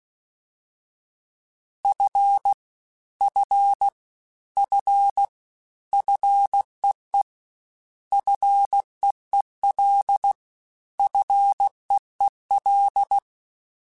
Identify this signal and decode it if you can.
Morse code, heard as a keyed tone, about 16 wpm, 800 Hz. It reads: F F F FEE FEEL FEEL
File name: Short Runs / Short Runs-mp3s / SinglePrefixes @15